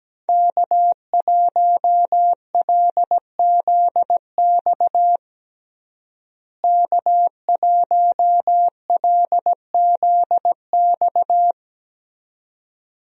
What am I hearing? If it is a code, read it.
Morse code, heard as a keyed tone, about 17 wpm, 700 Hz. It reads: K1LZX K1LZX
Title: USA Call Signs - 5 Characters - Repeated 2x - 17wpm